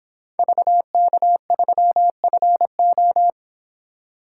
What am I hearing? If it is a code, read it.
Morse code, heard as a keyed tone, about 26 wpm, 700 Hz. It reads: VK3FO
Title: Speed Racing - USA 2x2 Call Signs 22wpm